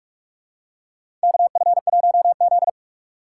Morse code, heard as a keyed tone, about 47 wpm, 690 Hz. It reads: KF1Z
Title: USA Call Signs - 4 Characters - 4x Character Spacing - 45wpm